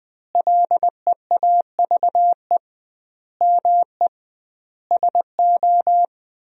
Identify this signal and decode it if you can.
Morse code, heard as a keyed tone, about 20 wpm, 700 Hz. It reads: LEAVE ME SO
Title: 2x Word-Spacing - Sentences from Top 400 Words - 20wpm